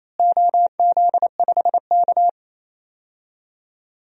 Morse code, heard as a keyed tone, about 28 wpm, 700 Hz. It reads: OZ5K
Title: DXCC-100 Call Signs - 4 Characters - Repeated 3x - 28wpm